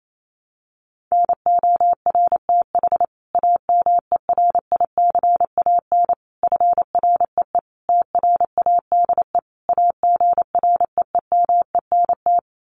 Morse code, heard as a keyed tone, about 28 wpm, 700 Hz. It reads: NORTH AMERICAN FREE TRADE AGREEMENT
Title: Sets of 5 Words Encore 28wpm